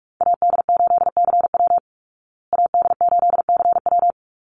Morse code, heard as a keyed tone, about 44 wpm, 700 Hz. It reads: AD8CW AD8CW
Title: Call Signs Repeated 3x - 45wpm